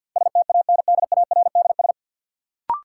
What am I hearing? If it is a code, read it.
Morse code, heard as a keyed tone, about 50 wpm, 700 Hz. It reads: STANDARDS
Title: Sentences from Top 1000 Words - Review Part 6 - New Word Drill - 50wpm